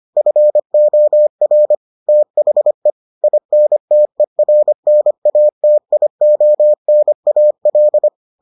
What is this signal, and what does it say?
Morse code, heard as a keyed tone, about 25 wpm, 600 Hz. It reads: FOR THE INTERNATIONAL